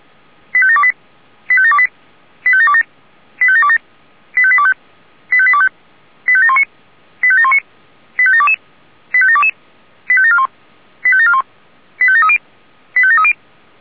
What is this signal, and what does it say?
Five-tone selective call (ZVEI), heard as quick runs of five tones, tones of about 70 ms. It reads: 85728 85728 85727 85727 85825 85825 85719 85719 85722 85722 85741 85741 85730 85730